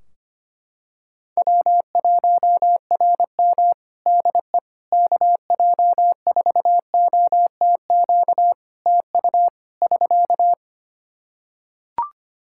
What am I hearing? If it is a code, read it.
Morse code, heard as a keyed tone, about 25 wpm, 700 Hz. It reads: W1RM DE KJ4OTQ TU <SK>
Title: Practice QSO - Exchange 4 - Ending - 25wpm